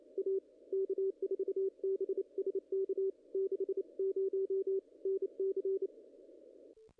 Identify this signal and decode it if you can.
Morse code, heard as a keyed tone, about 28 wpm, 440 Hz, with radio noise. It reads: A K4BSK 60 NC
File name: contests / ss-cw-k5tr-op-n5rz -0000